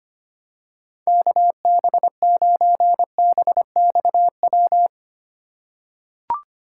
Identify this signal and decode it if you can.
Morse code, heard as a keyed tone, about 25 wpm, 700 Hz. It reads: KB9BXW
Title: USA Call Signs - 6 Characters - 2x Character Spacing - 25wpm